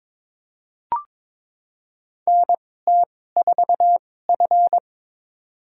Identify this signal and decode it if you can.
Morse code, heard as a keyed tone, about 21 wpm, spaced as for 16 wpm, 700 Hz. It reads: NT4F
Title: USA Call Signs - 4 Characters - 2x Character Spacing - 22wpm